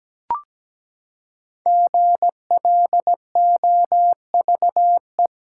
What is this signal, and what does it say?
Morse code, heard as a keyed tone, about 17 wpm, 700 Hz. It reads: GLOVE
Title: Five Letter Words - 17wpm